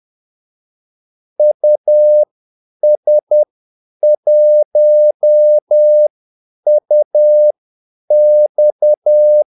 Morse code, heard as a keyed tone, about 10 wpm, spaced as for 8 wpm, 600 Hz. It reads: US1UX
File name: Callsigns-10-8wpm